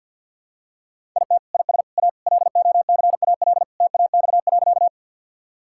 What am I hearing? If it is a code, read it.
Morse code, heard as a keyed tone, about 51 wpm, 700 Hz. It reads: IT IS A LOCAL TAX.
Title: Sentences from Top 1000 Words - Review Part 5 - 50wpm